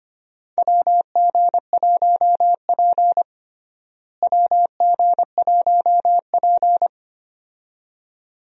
Morse code, heard as a keyed tone, about 25 wpm, 700 Hz. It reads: WG1P WG1P
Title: USA Call Signs - 4 Characters - Repeated 2x - 25wpm